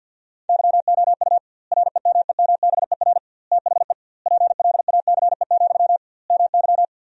Morse code, heard as a keyed tone, about 50 wpm, 700 Hz. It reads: YOU REMEMBER THE PLACE, MY
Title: 2x Word-Spacing - Sentences from Top 400 Words - 50wpm